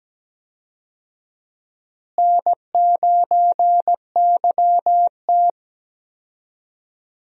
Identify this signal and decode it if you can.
Morse code, heard as a keyed tone, about 17 wpm, 700 Hz. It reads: N9YT
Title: USA Call Signs - 4 Characters - Repeated 2x - 17wpm